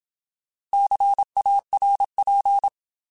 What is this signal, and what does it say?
Morse code, heard as a keyed tone, about 27 wpm, 800 Hz. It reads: CARP
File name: Alphabetical